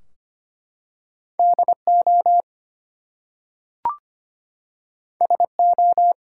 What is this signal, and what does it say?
Morse code, heard as a keyed tone, about 25 wpm, 700 Hz. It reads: DO SO
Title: Two Letter Words - 25wpm